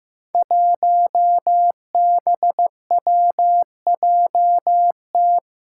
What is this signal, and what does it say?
Morse code, heard as a keyed tone, about 15 wpm, 700 Hz. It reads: 1BWJT
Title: Five Challenging Letter-Number Combinations 15wpm